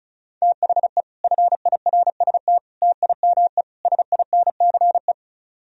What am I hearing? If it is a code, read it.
Morse code, heard as a keyed tone, about 43 wpm, 700 Hz. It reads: THE FIRST TIME SINCE
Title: Speed-Racing - Sets of 4 Words 35wpm